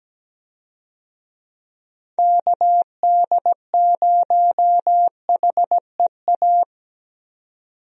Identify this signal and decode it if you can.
Morse code, heard as a keyed tone, about 17 wpm, 700 Hz. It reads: KD0HEA
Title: USA Call Signs - 6 Characters - 17wpm